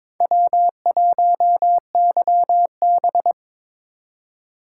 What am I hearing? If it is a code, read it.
Morse code, heard as a keyed tone, about 22 wpm, 700 Hz. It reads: W1YB
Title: USA Call Signs - 3x Character Spacing - 22wpm